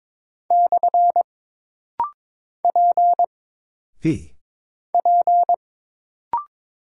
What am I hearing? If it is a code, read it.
Morse code, heard as a keyed tone, about 22 wpm, 700 Hz. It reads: / P P
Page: This rapid-fire timing is geared to push your ICR (Instant Character Recognition) skills.